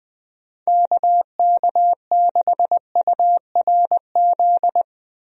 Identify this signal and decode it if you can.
Morse code, heard as a keyed tone, about 20 wpm, 700 Hz. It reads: KK6URZ
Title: USA Call Signs - 6 Characters - Repeated 3x - 20wpm